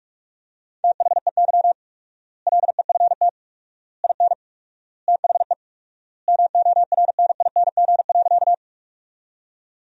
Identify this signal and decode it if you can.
Morse code, heard as a keyed tone, about 46 wpm, 700 Hz. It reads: THEY LEFT IN THE MORNING.
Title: 4x Word-Spacing - Sentences from Top 400 Words - 45wpm